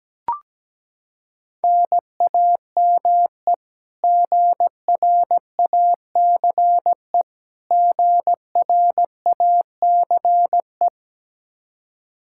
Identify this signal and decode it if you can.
Morse code, heard as a keyed tone, about 17 wpm, 700 Hz. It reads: NAME GRACE GRACE
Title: Practice QSO - Exchange 1 - Beginning - 17wpm